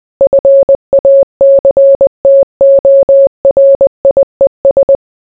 Morse code, heard as a keyed tone, about 20 wpm, 550 Hz. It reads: FACTORIES